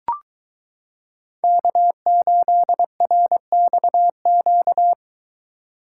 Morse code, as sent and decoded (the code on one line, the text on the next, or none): -.- ---.. .-. -..- --.-
K8RXQ